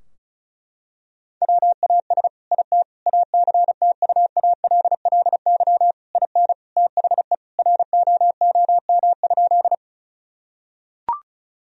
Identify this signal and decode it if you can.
Morse code, heard as a keyed tone, about 34 wpm, 700 Hz. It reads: WAS IT ACTUALLY IN THE ROOM?
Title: Sentences from Top 1000 Words - Review Part 4 - 35wpm